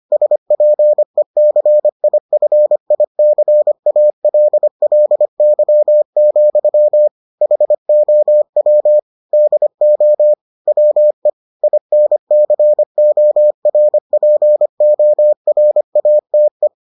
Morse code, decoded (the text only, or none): SPECIFICALLY, HOW DO WE INCORPORATE